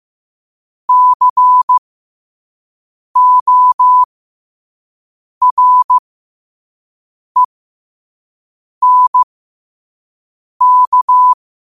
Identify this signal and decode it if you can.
Morse code, heard as a keyed tone, about 15 wpm, spaced as for 5 wpm, 1000 Hz. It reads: CORENK